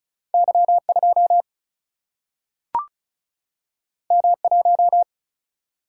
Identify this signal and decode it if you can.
Morse code, heard as a keyed tone, about 36 wpm, 700 Hz. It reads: Y 2 M 1
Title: Single-Letter Single-Number 35wpm